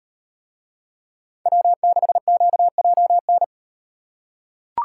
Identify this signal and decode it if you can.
Morse code, heard as a keyed tone, about 35 wpm, 700 Hz. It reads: W6QJN